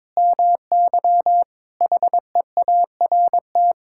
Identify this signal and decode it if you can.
Morse code, heard as a keyed tone, about 22 wpm, 700 Hz. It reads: MY HEART
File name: practice / 22wpm / Sets of 6 Words - 22wpm